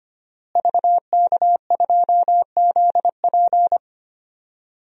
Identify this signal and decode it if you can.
Morse code, heard as a keyed tone, about 25 wpm, 700 Hz. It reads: VK2ZP
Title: DXCC-100 Call Signs - 5 Characters - Repeated 2x - 25wpm